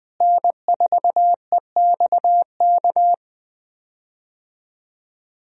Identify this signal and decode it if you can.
Morse code, heard as a keyed tone, about 20 wpm, 700 Hz. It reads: N4EXK
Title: USA Call Signs - 5 Characters - Repeated 2x - 20wpm